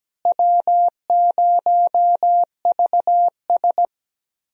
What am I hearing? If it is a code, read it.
Morse code, heard as a keyed tone, about 17 wpm, 700 Hz. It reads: W0VS
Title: USA Call Signs - 4 Characters - Repeated 2x - 17wpm